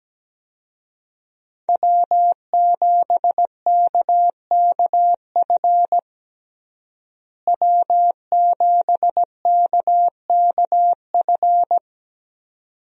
Morse code, heard as a keyed tone, about 17 wpm, 700 Hz. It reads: W7KKF W7KKF
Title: USA Call Signs - 5 Characters - Repeated 2x - 17wpm